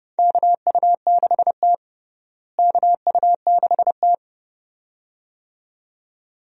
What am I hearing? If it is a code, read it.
Morse code, heard as a keyed tone, about 30 wpm, 700 Hz. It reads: KU6T KU6T
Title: USA Call Signs - 4 Characters - Repeated 2x - 30wpm